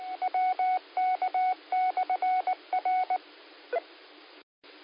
Morse code, heard as a keyed tone, about 19 wpm, 740 Hz. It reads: WK/R